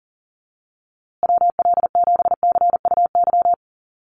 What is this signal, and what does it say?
Morse code, heard as a keyed tone, about 38 wpm, 700 Hz. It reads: WL7CUY
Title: USA Call Signs Repeated 2x 40wpm